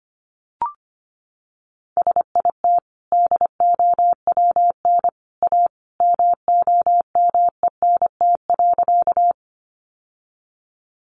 Morse code, heard as a keyed tone, about 25 wpm, 700 Hz. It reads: SIT DOWN A MOMENT.
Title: Sentences from Top 1000 Words 25wpm